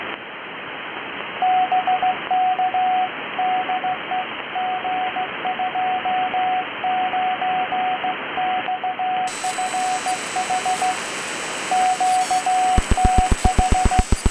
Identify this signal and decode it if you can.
Morse code, heard as a keyed tone, about 15 wpm, 740 Hz, with radio noise. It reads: BK DE G29XFH QTH